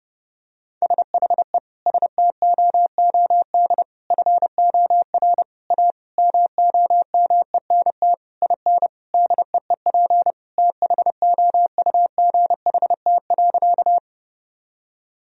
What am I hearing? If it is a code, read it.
Morse code, heard as a keyed tone, about 30 wpm, 700 Hz. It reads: SHE STOOD FOR A MOMENT IN DEEP THOUGHT.
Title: Sentences from Top 1000 Words - Review Part 3 - 30wpm